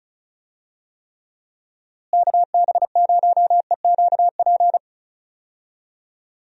Morse code, heard as a keyed tone, about 36 wpm, 700 Hz. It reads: KB0EQP
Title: USA Call Signs - 6 Characters - Repeated 2x - 35wpm